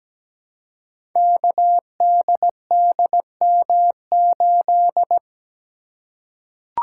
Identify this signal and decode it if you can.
Morse code, heard as a keyed tone, about 17 wpm, 700 Hz. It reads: KDDM8
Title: Code Groups - 5 Letter:Numbers - 17wpm